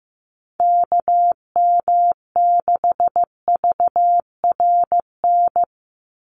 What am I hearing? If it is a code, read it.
Morse code, heard as a keyed tone, about 15 wpm, 700 Hz. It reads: KM6VRN
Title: USA Call Signs 15wpm